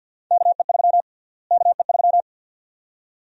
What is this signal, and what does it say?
Morse code, heard as a keyed tone, about 50 wpm, 700 Hz. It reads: KE3 KE3
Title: USA Call Sign Prefixes - Repeated 2x - 50wpm